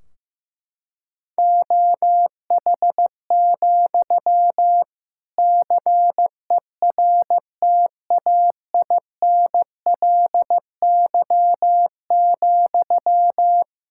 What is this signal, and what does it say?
Morse code, heard as a keyed tone, about 15 wpm, 700 Hz. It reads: OH, CERTAINLY,
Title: Sentences from Top 1600 Words 15wpm